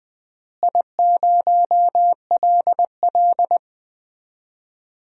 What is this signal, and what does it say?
Morse code, heard as a keyed tone, about 20 wpm, 700 Hz. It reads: I0LL
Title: DXCC-100 Call Signs - 4 Characters - Repeated 2x - 20wpm